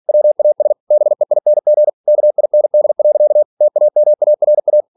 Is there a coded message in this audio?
Yes: Morse code from 0.1 to 4.8 s, heard as a keyed tone, about 46 wpm, 600 Hz. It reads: WAS BEING KIND. TAMARA